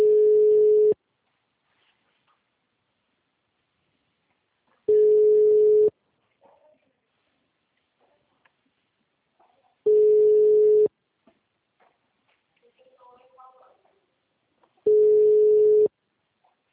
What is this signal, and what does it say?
Ringback tone (Europe, 425 Hz), with speech.